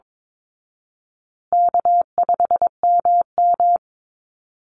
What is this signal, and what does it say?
Morse code, heard as a keyed tone, about 22 wpm, 700 Hz. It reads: K5MM
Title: Call Signs Repeated 2x - 22wpm